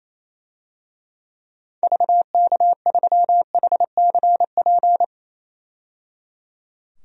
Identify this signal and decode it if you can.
Morse code, heard as a keyed tone, about 28 wpm, 700 Hz. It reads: VK3HCP
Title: DXCC-100 Call Signs - 6 Characters - 28wpm